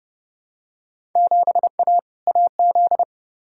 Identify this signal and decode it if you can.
Morse code, heard as a keyed tone, about 30 wpm, 700 Hz. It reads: 7A AZ
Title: Field Day Exchanges - 30wpm